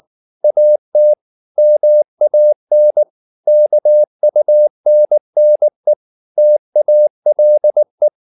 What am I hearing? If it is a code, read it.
Morse code, heard as a keyed tone, about 19 wpm, 600 Hz. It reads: AT MAN KUNNE TALE
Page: Ferie 19 wpm